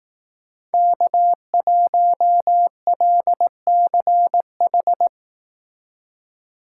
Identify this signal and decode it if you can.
Morse code, heard as a keyed tone, about 18 wpm, 700 Hz. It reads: K1LCH
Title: Speed Racing - USA 1x3 Call Signs 15wpm